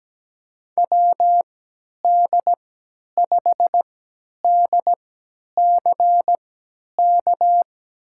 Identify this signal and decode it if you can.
Morse code, heard as a keyed tone, about 17 wpm, spaced as for 10 wpm, 700 Hz. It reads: WD5DCK
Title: USA Call Signs - 3x Character Spacing - 17wpm